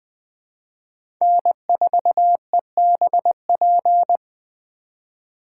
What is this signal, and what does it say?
Morse code, heard as a keyed tone, about 20 wpm, 700 Hz. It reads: N4EBP
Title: USA Call Signs - 5 Characters - 2x Character Spacing - 20wpm